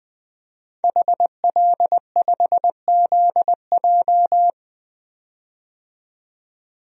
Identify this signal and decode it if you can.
Morse code, heard as a keyed tone, about 20 wpm, 700 Hz. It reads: HL5ZJ
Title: DXCC-100 Call Signs - 5 Characters - 20wpm